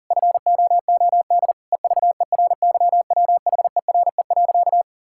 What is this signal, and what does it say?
Morse code, heard as a keyed tone, about 37 wpm, 700 Hz. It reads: FOOD EVERYWHERE.
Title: Sentences from Top 1600 Words 40wpm